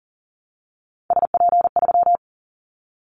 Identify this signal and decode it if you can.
Morse code, heard as a keyed tone, about 38 wpm, 700 Hz. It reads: SP3
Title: Callsign Prefixes for Top 25 DXCC - 40wpm